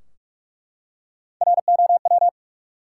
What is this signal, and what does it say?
Morse code, heard as a keyed tone, about 43 wpm, 700 Hz. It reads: ROW
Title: Three Letter Words - 45wpm